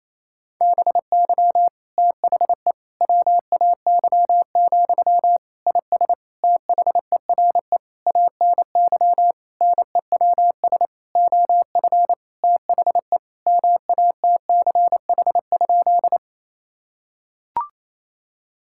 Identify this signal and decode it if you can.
Morse code, heard as a keyed tone, about 28 wpm, 700 Hz. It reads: BY THE WAY, IS THERE ANY NEWS OF THE MATCH?
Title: Sentences from Top 1600 Words 28wpm